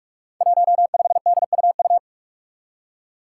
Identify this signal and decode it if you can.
Morse code, heard as a keyed tone, about 41 wpm, 700 Hz. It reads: 15DUU
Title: Five Challenging Letter-Number Combinations 45wpm